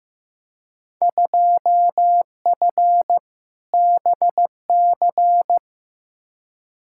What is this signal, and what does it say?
Morse code, heard as a keyed tone, about 15 wpm, 700 Hz. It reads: 2F BC
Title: Field Day Exchanges Encore - 15wpm